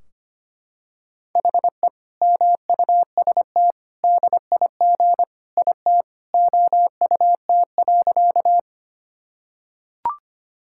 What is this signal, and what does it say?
Morse code, heard as a keyed tone, about 25 wpm, 700 Hz. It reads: HE MUST DIG IT OUT.
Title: Sentences from Top 1600 Words 25wpm